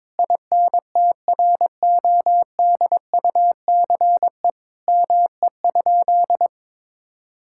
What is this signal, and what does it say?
Morse code, heard as a keyed tone, about 21 wpm, 700 Hz. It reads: INTRODUCE ME?
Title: Sentences from Top 1600 Words 22wpm